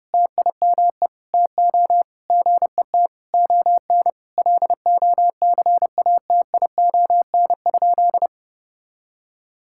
Morse code, heard as a keyed tone, about 30 wpm, 700 Hz. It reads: TIME TO GET ON LOCATION?
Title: Sentences from Top 1000 Words - Review Part 6 - 30wpm